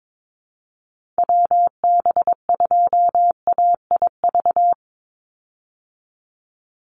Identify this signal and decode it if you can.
Morse code, heard as a keyed tone, about 22 wpm, 700 Hz. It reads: WB2AIV
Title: Call Signs Repeated 3x - 22wpm